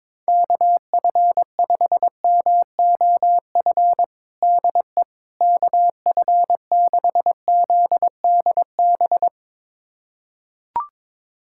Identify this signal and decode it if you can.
Morse code, heard as a keyed tone, about 22 wpm, 700 Hz. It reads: KF5MOF DE KF6ZDB
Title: USA Callsign DE Callsign - 6 Characters - 22wpm